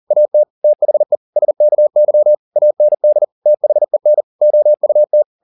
Morse code, heard as a keyed tone, about 40 wpm, 600 Hz. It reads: AT THE SKY AND THEN OUT
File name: the old man and the sea 2 40wpm